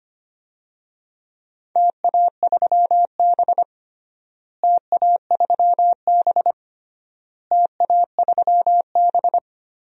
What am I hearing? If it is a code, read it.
Morse code, heard as a keyed tone, about 25 wpm, 700 Hz. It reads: TA3B TA3B TA3B
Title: DXCC-100 Call Signs - 4 Characters - Repeated 3x - 25wpm